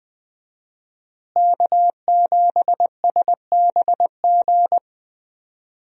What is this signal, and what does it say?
Morse code, heard as a keyed tone, about 20 wpm, 700 Hz. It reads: K7SBG
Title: USA Call Signs - 5 Characters - Repeated 2x - 20wpm